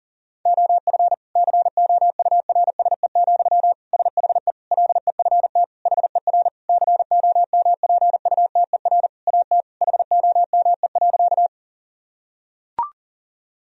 Morse code, heard as a keyed tone, about 40 wpm, 700 Hz. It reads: OF COURSE, SHE LEFT HER COMPUTER AT HOME.
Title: Sentences from Top 1000 Words - Review Part 3 - 40wpm